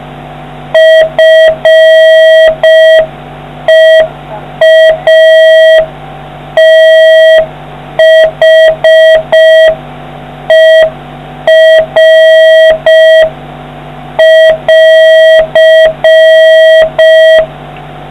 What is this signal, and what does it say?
Morse code, heard as a keyed tone, about 5 wpm, 1950 Hz, with noise. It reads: FEATHER<AR>